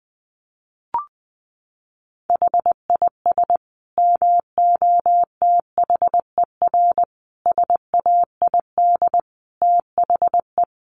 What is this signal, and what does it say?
Morse code, heard as a keyed tone, about 20 wpm, 700 Hz. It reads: HIS MOTHER SAID THE
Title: Sentences from Top 800 Words 20wpm